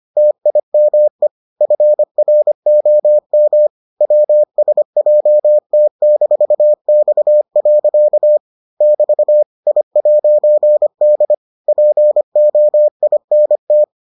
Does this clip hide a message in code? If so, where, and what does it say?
Yes: Morse code from 0.2 to 13.8 s, heard as a keyed tone, about 25 wpm, 600 Hz. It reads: TIME FROM WSJT-X. <BT> I'D POINT